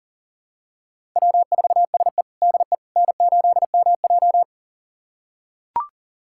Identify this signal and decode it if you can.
Morse code, heard as a keyed tone, about 37 wpm, 700 Hz. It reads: W4SE DE N8MJ
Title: USA Callsign DE Callsign - 4 Characters - 40wpm